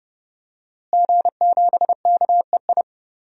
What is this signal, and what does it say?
Morse code, heard as a keyed tone, about 30 wpm, 700 Hz. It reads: G7KEI
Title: DXCC-100 Call Signs - 5 Characters - Repeated 3x - 30wpm